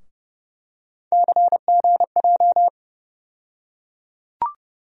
Morse code, heard as a keyed tone, about 30 wpm, 700 Hz. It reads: CGJ
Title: USA Call Sign Postfixes - 3 Letters 30wpm